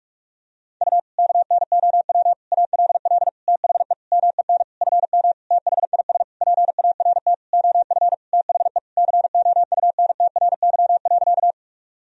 Morse code, heard as a keyed tone, about 46 wpm, 700 Hz. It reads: U KNOW ALL THE MEN FM THIS PART OF THE COUNTRY.
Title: Sentences from Top 300 Words and Abbreviations 45wpm